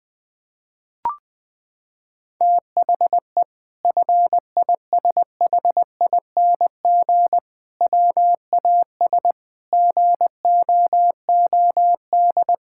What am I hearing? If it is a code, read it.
Morse code, heard as a keyed tone, about 21 wpm, 700 Hz. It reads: THE FISHING WAS GOOD
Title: Sentences from Top 1000 Words - Review Part 5 - 20wpm